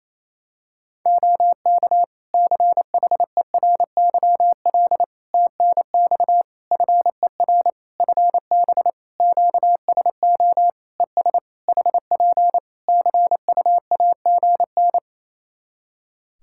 Morse code, heard as a keyed tone, about 28 wpm, 700 Hz. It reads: OK CHERYL TNX FER FB QSO ES HP CUAGN
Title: Practice QSO - Exchange 4 - Ending - 28wpm